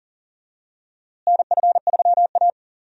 Morse code, heard as a keyed tone, about 40 wpm, 700 Hz. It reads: NF3A
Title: USA Call Signs - 4 Characters - Repeated 2x - 40wpm